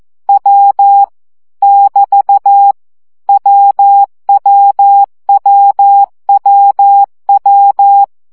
Morse code, heard as a keyed tone, about 14 wpm, 800 Hz. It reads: W <BT> WWWWW
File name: CW gyakorlas w 12 baud